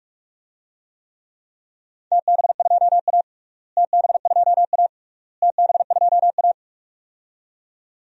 Morse code, heard as a keyed tone, about 41 wpm, 690 Hz. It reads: TB2A TB2A TB2A
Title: DXCC-100 Call Signs - 4 Characters - Repeated 3x - 45wpm